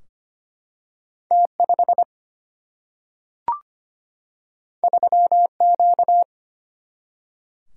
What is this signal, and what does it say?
Morse code, heard as a keyed tone, about 25 wpm, 700 Hz. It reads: T5 3Q
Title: Single-Letter Single-Number 25wpm